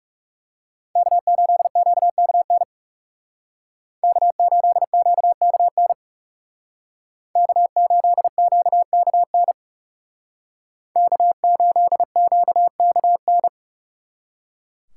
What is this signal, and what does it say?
Morse code, heard as a keyed tone, about 36 wpm, 690 Hz. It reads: K8QKN K8QKN K8QKN K8QKN
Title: Speed Racing - USA Call Signs 30wpm